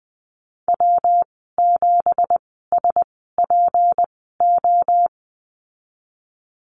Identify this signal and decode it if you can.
Morse code, heard as a keyed tone, about 20 wpm, spaced as for 14 wpm, 700 Hz. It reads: W7SPO
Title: 179 Lesson 29g - Slash - Call Signs 20wpm